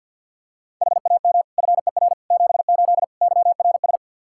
Morse code, heard as a keyed tone, about 52 wpm, 700 Hz. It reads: HAM FER 78 YRS
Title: Practice QSO - Exchange 3 - More Info - 50wpm